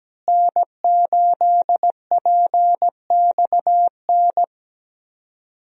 Morse code, heard as a keyed tone, about 17 wpm, 700 Hz. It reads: N8PXN
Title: USA Call Signs - 5 Characters - Repeated 3x - 17wpm